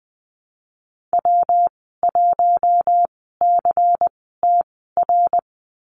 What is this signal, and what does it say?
Morse code, heard as a keyed tone, about 20 wpm, spaced as for 14 wpm, 700 Hz. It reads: W1CTR
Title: 140 Lesson 23g - Letter f - Call Signs 20wpm